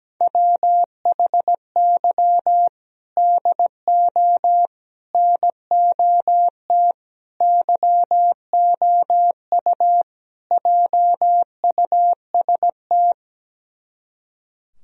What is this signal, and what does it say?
Morse code, heard as a keyed tone, about 17 wpm, 700 Hz. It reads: WHY DO NOT YOU JUST
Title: Speed-Racing - Sets of 5 Words 17wpm